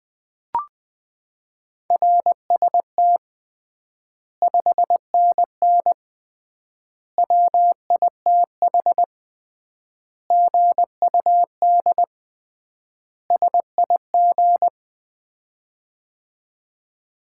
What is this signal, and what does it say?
Morse code, heard as a keyed tone, about 20 wpm, 700 Hz. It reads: RST 5NN WITH GUD SIG
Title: Common QSO Phrases - 3x Word-Spacing - 20wpm